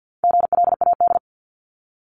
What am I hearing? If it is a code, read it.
Morse code, heard as a keyed tone, about 50 wpm, 700 Hz. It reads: GLAD
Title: Multi-letter Phonemes Repeated 3x 50wpm